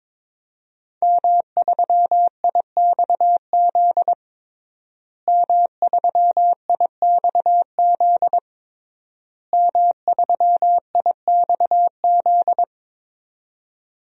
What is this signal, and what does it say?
Morse code, heard as a keyed tone, about 22 wpm, 700 Hz. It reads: M3IXZ M3IXZ M3IXZ
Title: DXCC-100 Call Signs - 5 Characters - Repeated 3x - 22wpm